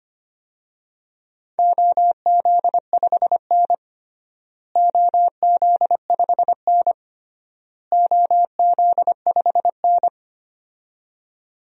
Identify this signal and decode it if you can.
Morse code, heard as a keyed tone, about 25 wpm, 700 Hz. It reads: OZ5N OZ5N OZ5N
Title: DXCC-100 Call Signs - 4 Characters - Repeated 3x - 25wpm